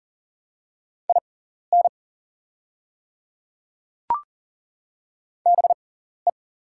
Morse code, heard as a keyed tone, about 38 wpm, spaced as for 14 wpm, 700 Hz. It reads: IN BE
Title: 6x Character Spacing - Top 500 Words - 40wpm